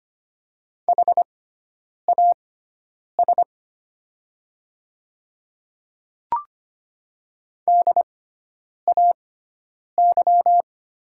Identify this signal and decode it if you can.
Morse code, heard as a keyed tone, about 25 wpm, spaced as for 9 wpm, 700 Hz. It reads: HAS DAY
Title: 6x Character Spacing - Top 100 Words - 25wpm